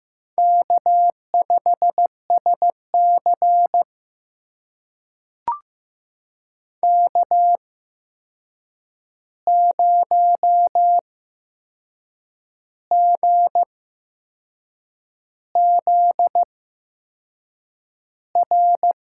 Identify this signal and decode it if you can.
Morse code, heard as a keyed tone, about 15 wpm, spaced as for 6 wpm, 700 Hz. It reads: K5SC K0GZR
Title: USA Call Signs - 8x Character Spacing - 15wpm